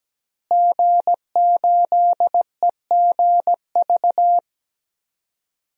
Morse code, heard as a keyed tone, about 17 wpm, 700 Hz. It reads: G8EGV
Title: DXCC-100 Call Signs - 5 Characters - Repeated 2x - 17wpm